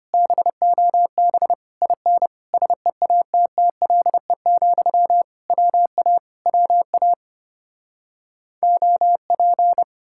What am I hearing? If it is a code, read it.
Morse code, heard as a keyed tone, about 33 wpm, 700 Hz. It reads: BOB IN SEATTLE, WA WA OP